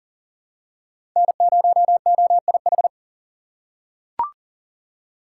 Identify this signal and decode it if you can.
Morse code, heard as a keyed tone, about 40 wpm, 700 Hz. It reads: N0OIH
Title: USA Call Signs - 5 Characters - 6x Character Spacing - 40wpm